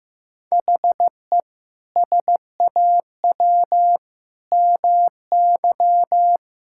Morse code, heard as a keyed tone, about 15 wpm, 700 Hz. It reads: HE SAW MY
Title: Sentences from Top 1600 Words 15wpm